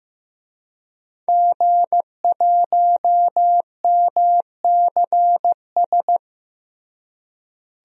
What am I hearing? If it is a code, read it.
Morse code, heard as a keyed tone, about 15 wpm, 700 Hz. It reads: G1MCS
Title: DXCC-100 Call Signs - 5 Characters - Repeated 3x - 15wpm